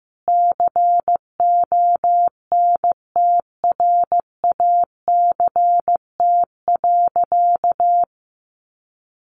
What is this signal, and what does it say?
Morse code, heard as a keyed tone, about 15 wpm, 700 Hz. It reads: CONTRACT.
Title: Sentences from Top 1400 Words 15wpm